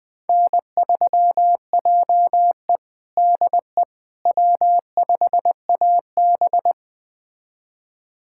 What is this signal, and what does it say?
Morse code, heard as a keyed tone, about 20 wpm, 700 Hz. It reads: N3JE DE W5AB
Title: USA Callsign DE Callsign - 4 Characters - 20wpm